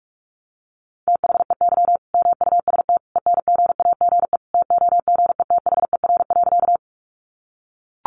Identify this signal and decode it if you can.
Morse code, heard as a keyed tone, about 44 wpm, 700 Hz. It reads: THEY MUST ENGAGE TOGETHER.
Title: Sentences from Top 1500 Words 45wpm